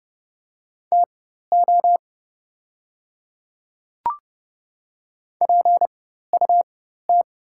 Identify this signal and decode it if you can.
Morse code, heard as a keyed tone, about 30 wpm, spaced as for 14 wpm, 690 Hz. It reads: TO PUT
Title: 4x Character Spacing - Top 200 Words - 30wpm